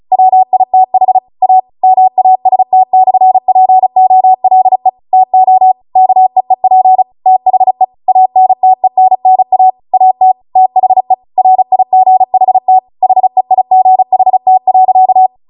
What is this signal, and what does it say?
Morse code, heard as a keyed tone, about 36 wpm, 750 Hz. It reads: WITH A MAST/POLE TO KEEP THE ANTENNA AT THE RIGHT HEIGHT.
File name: Morse / 35WPM